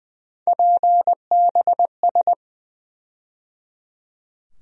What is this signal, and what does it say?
Morse code, heard as a keyed tone, about 20 wpm, 700 Hz. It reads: PBS
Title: USA Call Sign Postfixes - 3 Letters 20wpm